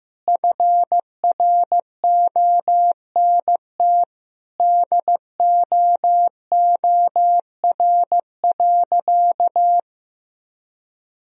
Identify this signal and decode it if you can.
Morse code, heard as a keyed tone, about 15 wpm, 700 Hz. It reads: FRONT DOOR.
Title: Sentences from Top 1000 Words - Review Part 2 - 15wpm